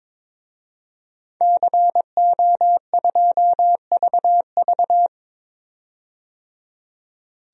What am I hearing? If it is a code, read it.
Morse code, heard as a keyed tone, about 22 wpm, 700 Hz. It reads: CO2VV
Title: DXCC-100 Call Signs - 5 Characters - 22wpm